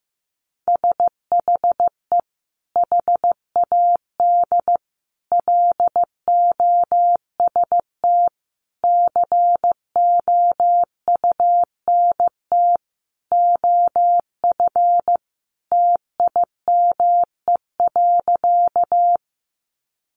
Morse code, heard as a keyed tone, about 15 wpm, 700 Hz. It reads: SHE HAD LOST COUNT OF TIME.
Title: Sentences from Top 1500 Words 15wpm